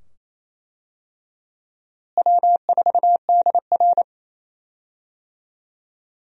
Morse code, heard as a keyed tone, about 27 wpm, 700 Hz. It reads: W4DR
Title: USA Call Signs - 4 Characters - Repeated 2x - 28wpm